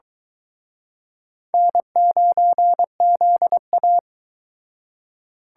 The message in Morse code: N9ZA

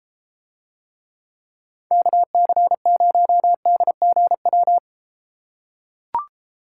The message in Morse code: KC0DGW

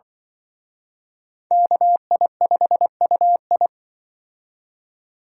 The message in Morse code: KI5UI

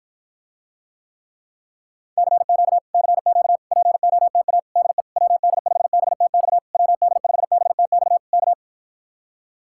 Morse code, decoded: CQ CQ POTA DE WD5BTX WD5BTX K

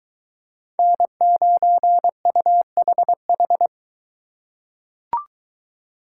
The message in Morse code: N9UHH

